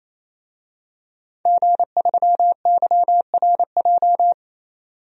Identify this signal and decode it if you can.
Morse code, heard as a keyed tone, about 28 wpm, 700 Hz. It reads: G3YRJ
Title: DXCC-100 Call Signs - 5 Characters - 28wpm